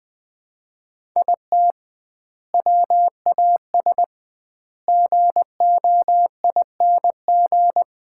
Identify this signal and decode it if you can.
Morse code, heard as a keyed tone, about 21 wpm, 700 Hz. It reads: IT WAS GOING
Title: 2x Word-Spacing - Sets of 5 Words - 20wpm